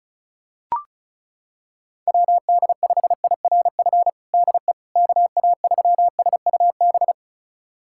Morse code, heard as a keyed tone, about 36 wpm, 700 Hz. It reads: WD5IRF DE KA3SUB
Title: USA Callsign DE Callsign - 6 Characters - 35wpm